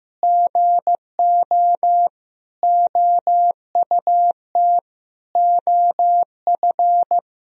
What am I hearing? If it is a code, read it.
Morse code, heard as a keyed tone, about 15 wpm, 700 Hz. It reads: GO OUT OF
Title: Sentences from Top 1000 Words - Review Part 4 - 15wpm